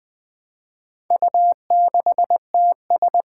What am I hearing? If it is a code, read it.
Morse code, heard as a keyed tone, about 20 wpm, 700 Hz. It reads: U6TS
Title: Four Challenging Letter-Number Combinations 20wpm